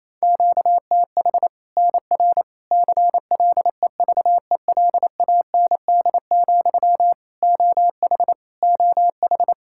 Morse code, heard as a keyed tone, about 28 wpm, 700 Hz. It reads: QTH NR CLEVELAND, OH OH
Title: Practice QSO - Exchange 1 - Beginning - 28wpm